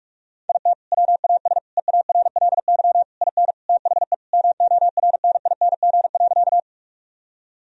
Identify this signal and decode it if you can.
Morse code, heard as a keyed tone, about 46 wpm, 700 Hz. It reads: IT WAS EARLY IN THE MORNING.
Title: Sentences from Top 1000 Words - Review Part 2 - 45wpm